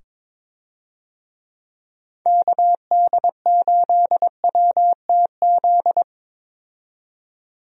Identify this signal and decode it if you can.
Morse code, heard as a keyed tone, about 22 wpm, 700 Hz. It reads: KD8WTZ
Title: USA Call Signs - 6 Characters - 22wpm